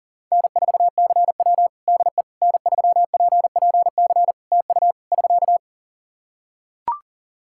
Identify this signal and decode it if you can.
Morse code, heard as a keyed tone, about 43 wpm, 700 Hz. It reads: N4CW DE N3PPC TU <SK>
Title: Practice QSO - Exchange 4 - Ending - 40wpm